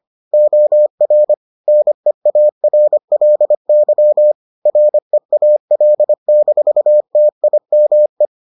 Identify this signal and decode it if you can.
Morse code, heard as a keyed tone, about 25 wpm, 600 Hz. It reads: OR NEARLY REAL-TIME